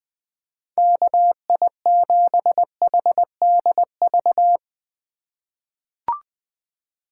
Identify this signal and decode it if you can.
Morse code, heard as a keyed tone, about 20 wpm, 700 Hz. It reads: KI7HDV
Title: USA Call Signs - 6 Characters - 3x Character Spacing - 20wpm